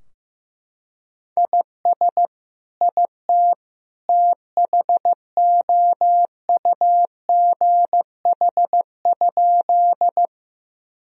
Morse code, heard as a keyed tone, about 15 wpm, 700 Hz. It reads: IS IT THOUGH?
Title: Sentences from Top 1000 Words - Review Part 2 - 15wpm